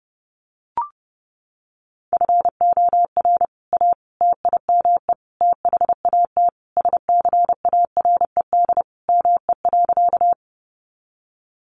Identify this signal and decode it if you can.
Morse code, heard as a keyed tone, about 30 wpm, 700 Hz. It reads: FOR A TIME THAT SCARED ME.
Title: Sentences from Top 700 Words 30wpm